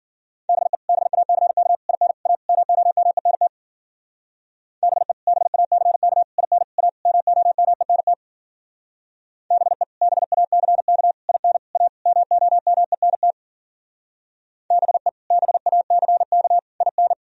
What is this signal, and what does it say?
Morse code, heard as a keyed tone, about 48 wpm, 700 Hz. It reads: BE BACK IN A MOMENT BE BACK IN A MOMENT BE BACK IN A MOMENT BE BACK IN
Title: Speed-Racing - Sets of 5 Words 40wpm